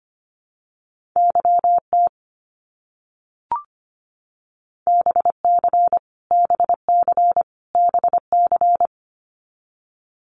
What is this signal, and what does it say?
Morse code, heard as a keyed tone, about 25 wpm, 700 Hz. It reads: YT BC BC BC
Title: Canadian Provinces and Territories - Repeated 3x - 25wpm